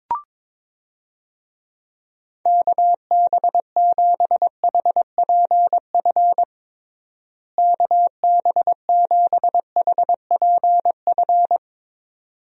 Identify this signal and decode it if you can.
Morse code, heard as a keyed tone, about 22 wpm, 700 Hz. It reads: KB7HPF KB7HPF
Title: USA Call Signs - 6 Characters - Repeated 2x - 22wpm